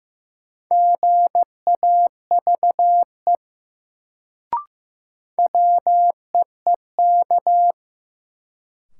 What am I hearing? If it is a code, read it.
Morse code, heard as a keyed tone, about 16 wpm, 700 Hz. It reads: GAVE WEEK
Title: Sentences from Top 500 Words - New Word Drill - 15wpm